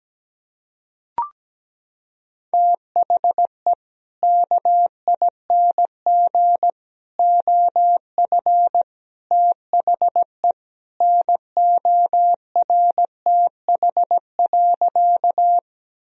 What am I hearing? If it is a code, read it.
Morse code, heard as a keyed tone, about 17 wpm, 700 Hz. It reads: THE KING OF THE NORTH.
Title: Sentences from Top 1000 Words - Review Part 1 - 17wpm